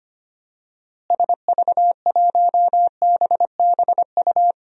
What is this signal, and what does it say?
Morse code, heard as a keyed tone, about 25 wpm, 700 Hz. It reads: SV1BBU
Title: DXCC-100 Call Signs - 6 Characters - Repeated 2x - 25wpm